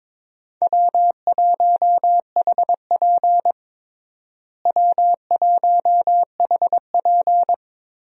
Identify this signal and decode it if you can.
Morse code, heard as a keyed tone, about 21 wpm, 700 Hz. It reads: W1HP W1HP
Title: USA Call Signs - 4 Characters - Repeated 3x - 22wpm